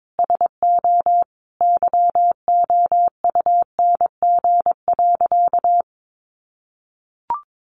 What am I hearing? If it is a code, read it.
Morse code, heard as a keyed tone, about 22 wpm, 700 Hz. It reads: SO YOUNG.
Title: Sentences from Top 400 Words 22wpm